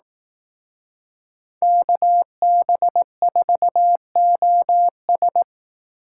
Morse code, heard as a keyed tone, about 18 wpm, 700 Hz. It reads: KB4OS
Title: Speed Racing - USA 2x2 Call Signs 15wpm